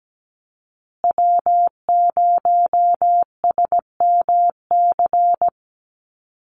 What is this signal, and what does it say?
Morse code, heard as a keyed tone, about 17 wpm, 700 Hz. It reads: W0SMC
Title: USA Call Signs Repeated 2x 17wpm